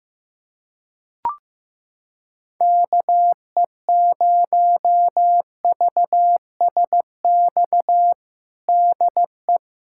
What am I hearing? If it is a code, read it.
Morse code, heard as a keyed tone, about 15 wpm, 700 Hz. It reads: KE0VSX DE
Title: USA Callsign DE Callsign 15wpm